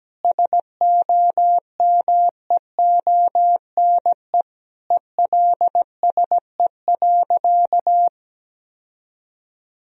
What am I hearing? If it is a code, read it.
Morse code, heard as a keyed tone, about 17 wpm, 700 Hz. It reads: SOMEONE ELSE.
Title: Sentences from Top 1000 Words - Review Part 4 - 17wpm